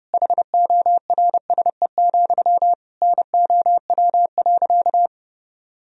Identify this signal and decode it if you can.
Morse code, heard as a keyed tone, about 32 wpm, 700 Hz. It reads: HORSE, NOW.
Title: Sentences from Top 1000 Words - Review Part 1 - 30wpm